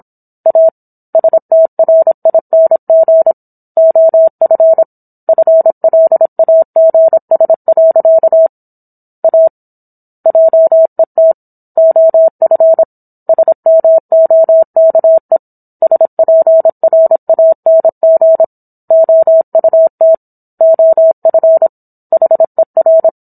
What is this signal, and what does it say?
Morse code, heard as a keyed tone, about 27 wpm, 650 Hz. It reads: A STRING OF FLAGS. A JET OF SMOKE SPRANG OUT OF HER